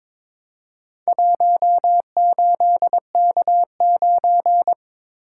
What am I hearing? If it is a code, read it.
Morse code, heard as a keyed tone, about 23 wpm, 700 Hz. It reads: 18K9
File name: practice / 22wpm / Four Letter-Number Combinations - 22wpm